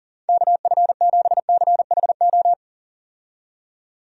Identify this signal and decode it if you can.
Morse code, heard as a keyed tone, about 40 wpm, 700 Hz. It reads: KF7CHO
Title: USA Call Signs - 6 Characters - 40wpm